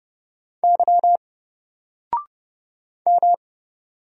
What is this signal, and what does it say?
Morse code, heard as a keyed tone, about 30 wpm, 700 Hz. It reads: Y M
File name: Single-Letter-Number-30wpm